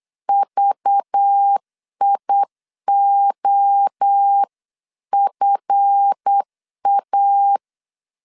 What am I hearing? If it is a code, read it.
Morse code, heard as a keyed tone, about 8 wpm, 800 Hz. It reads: VIOFA
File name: CW / DET / 1 9 1500C S C 8wpm 5653